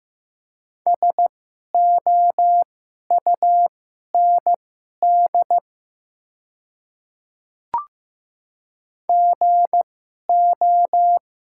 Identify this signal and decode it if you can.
Morse code, heard as a keyed tone, about 15 wpm, spaced as for 11 wpm, 700 Hz. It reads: SOUND GO
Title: 2x Character Spacing - Top 100 Words - 15wpm